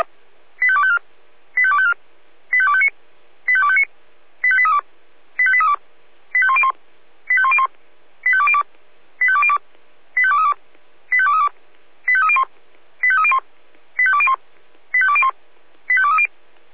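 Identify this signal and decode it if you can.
Five-tone selective call (ZVEI), heard as quick runs of five tones, tones of about 70 ms. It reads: 85254 85254 85259 85259 85832 85832 85101 85101 85202 85202 85232 85232 85301 85301 85201 85201 85230
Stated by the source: ZVEI five-tone selective call received via SDR.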